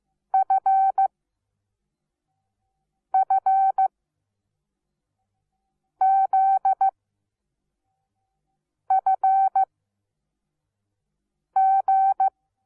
Morse code, heard as a keyed tone, about 15 wpm, spaced as for 4 wpm, 770 Hz. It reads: FFZFG
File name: Onlinelehrgang / morsen / DJ4UF Morsen17